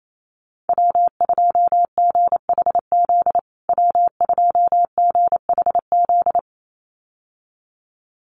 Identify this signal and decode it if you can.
Morse code, heard as a keyed tone, about 28 wpm, 700 Hz. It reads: W2GHZ W2GHZ
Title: USA Call Signs Repeated 2x 28wpm